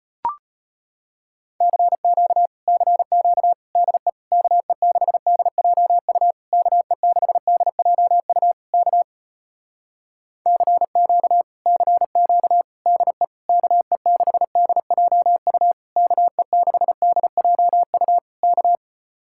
Morse code, heard as a keyed tone, about 33 wpm, 700 Hz. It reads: CQ CQ DE KE6DJU KE6DJU K CQ CQ DE KE6DJU KE6DJU K